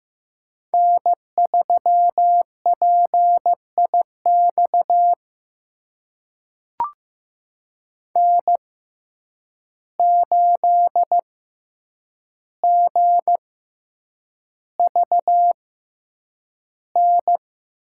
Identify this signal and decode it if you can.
Morse code, heard as a keyed tone, about 15 wpm, spaced as for 8 wpm, 700 Hz. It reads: N3PIX N8GVN
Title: USA Call Signs - 5 Characters - 6x Character Spacing - 15wpm